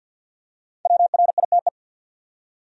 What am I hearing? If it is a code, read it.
Morse code, heard as a keyed tone, about 53 wpm, 700 Hz. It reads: WRITE
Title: Top 100 Words 50wpm